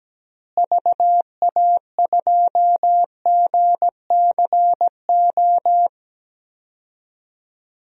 Morse code, heard as a keyed tone, about 17 wpm, 700 Hz. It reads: VA2GCO
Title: DXCC-100 Call Signs - 6 Characters - Repeated 2x - 17wpm